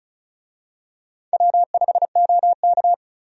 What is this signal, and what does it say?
Morse code, heard as a keyed tone, about 36 wpm, 700 Hz. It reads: W5OK
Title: USA Call Signs - 4 Characters - Repeated 2x - 35wpm